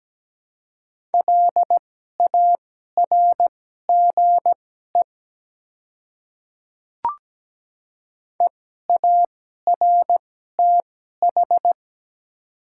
Morse code, heard as a keyed tone, about 17 wpm, spaced as for 12 wpm, 700 Hz. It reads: LARGE EARTH
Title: 2x Character Spacing - Top 300 Words - 17wpm